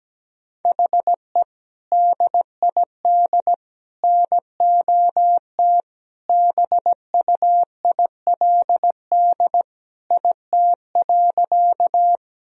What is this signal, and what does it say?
Morse code, heard as a keyed tone, about 17 wpm, 700 Hz. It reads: HE DID NOT BUILD IT.
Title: Sentences from Top 1000 Words - Review Part 1 - 17wpm